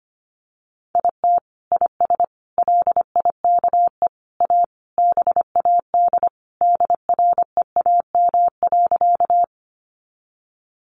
Morse code, heard as a keyed tone, about 25 wpm, 700 Hz. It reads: IT IS LIKE A BAD DREAM.
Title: Sentences from Top 1500 Words 25wpm